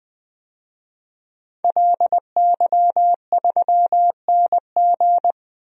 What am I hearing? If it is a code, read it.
Morse code, heard as a keyed tone, about 20 wpm, 700 Hz. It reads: LY3NG
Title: DXCC-100 Call Signs - 5 Characters - Repeated 2x - 20wpm